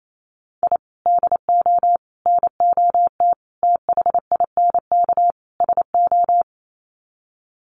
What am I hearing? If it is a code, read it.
Morse code, heard as a keyed tone, about 28 wpm, 700 Hz. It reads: I DO NOT THINK SO
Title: Sets of 5 Words 28wpm